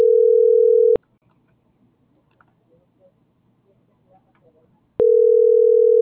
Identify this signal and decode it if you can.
Ringback tone (North America), 440 + 480 Hz.